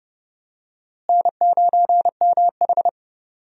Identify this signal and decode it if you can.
Morse code, heard as a keyed tone, about 30 wpm, 700 Hz. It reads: N9MH
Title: USA Call Signs - 4 Characters - Repeated 3x - 30wpm